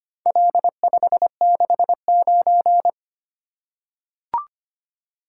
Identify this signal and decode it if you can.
Morse code, heard as a keyed tone, about 25 wpm, 700 Hz. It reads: L569
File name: practice / 25wpm / Four Letter-Number Combinations - 25wpm